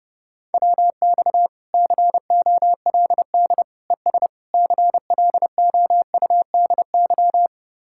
Morse code, heard as a keyed tone, about 32 wpm, 700 Hz. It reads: WX COLD ES CLOUDY